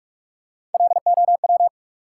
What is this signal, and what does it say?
Morse code, heard as a keyed tone, about 42 wpm, 700 Hz. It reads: LOW
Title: Three Letter Words - 45wpm